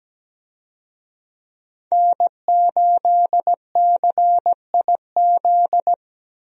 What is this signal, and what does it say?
Morse code, heard as a keyed tone, about 17 wpm, 700 Hz. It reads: N8CIZ
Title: USA Call Signs - 5 Characters - Repeated 2x - 17wpm